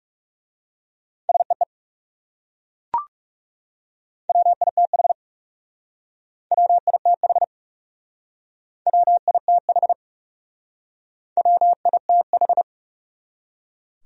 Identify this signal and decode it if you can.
Morse code, heard as a keyed tone, about 37 wpm, 700 Hz. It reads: SEE WITH WITH WITH WITH
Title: Speed-Racing Top 100 Words 30wpm